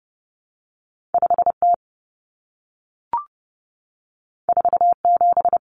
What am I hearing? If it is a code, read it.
Morse code, heard as a keyed tone, about 30 wpm, 700 Hz. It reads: 5T 47
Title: Common Cut Numbers - 2-Digit 30wpm